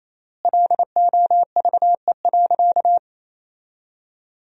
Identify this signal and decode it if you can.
Morse code, heard as a keyed tone, about 28 wpm, 700 Hz. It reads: LOVE.
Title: 3x Word-Spacing - Sentences from Top 400 Words - 28wpm